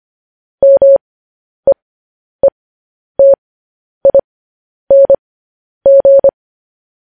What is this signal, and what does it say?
Morse code, heard as a keyed tone, about 25 wpm, spaced as for 10 wpm, 550 Hz. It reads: MEETING
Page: news-headlines-in-morse-code-at-10-wpm